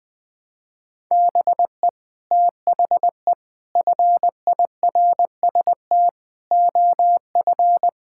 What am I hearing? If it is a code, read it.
Morse code, heard as a keyed tone, about 20 wpm, 700 Hz. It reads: BE THE FIRST OF
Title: Sets of 8 Words - 20wpm